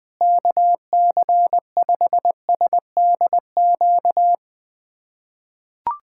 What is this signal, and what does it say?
Morse code, heard as a keyed tone, about 21 wpm, 700 Hz. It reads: KC5SDQ
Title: USA Callsign DE Callsign - 6 Characters - 20wpm